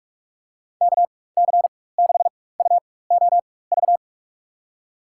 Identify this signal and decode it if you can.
Morse code, heard as a keyed tone, about 46 wpm, spaced as for 21 wpm, 700 Hz. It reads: KC6UOV
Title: USA Call Signs - 6 Characters - 4x Character Spacing - 45wpm